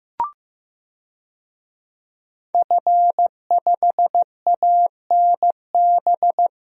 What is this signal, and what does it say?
Morse code, heard as a keyed tone, about 15 wpm, 700 Hz. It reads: F5ANB
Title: DXCC-100 Call Signs - 5 Characters - 15wpm